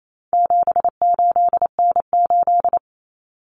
Morse code, heard as a keyed tone, about 28 wpm, 700 Hz. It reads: 78N8
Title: Common Cut Numbers - 4-Digit 28wpm